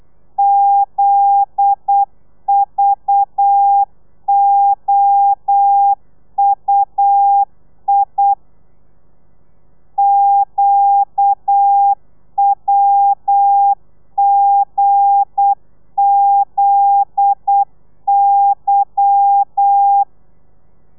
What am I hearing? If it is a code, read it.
Morse code, heard as a keyed tone, about 8 wpm, 800 Hz. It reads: ZVOUI QWGZY
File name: Lez11